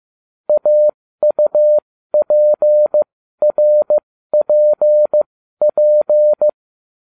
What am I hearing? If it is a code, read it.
Morse code, heard as a keyed tone, about 15 wpm, spaced as for 12 wpm, 600 Hz. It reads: AUPRPP
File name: cw / ubacw07 1